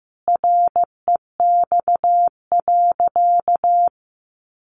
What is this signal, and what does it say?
Morse code, heard as a keyed tone, about 15 wpm, 700 Hz. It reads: REX.
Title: A Sound Of Thunder 15wpm